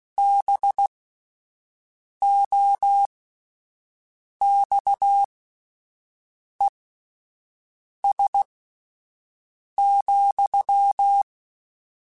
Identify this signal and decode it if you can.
Morse code, heard as a keyed tone, about 16 wpm, spaced as for 5 wpm, 800 Hz. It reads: BOXES,